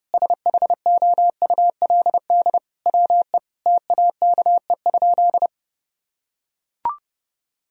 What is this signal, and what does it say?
Morse code, heard as a keyed tone, about 30 wpm, 700 Hz. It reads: SHOULD WE TAKE?
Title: Sentences from Top 1000 Words - Review Part 4 - 30wpm